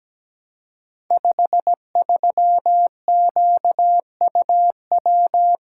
Morse code, heard as a keyed tone, about 17 wpm, 700 Hz. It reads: 53QUW
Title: Code Groups - 5 Letter:Numbers - 17wpm